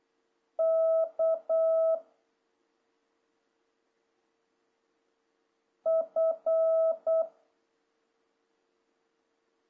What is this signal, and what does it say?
Morse code, heard as a keyed tone, about 8 wpm, 650 Hz. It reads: K F
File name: jh8cbh / 2026cw